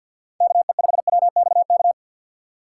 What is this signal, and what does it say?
Morse code, heard as a keyed tone, about 47 wpm, 700 Hz. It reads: KE5WXK
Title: USA Call Signs - 6 Characters - 6x Character Spacing - 50wpm